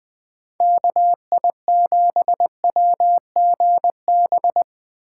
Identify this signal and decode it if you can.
Morse code, heard as a keyed tone, about 22 wpm, 700 Hz. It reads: KI7WGB
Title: Speed Racing - USA Call Signs 20wpm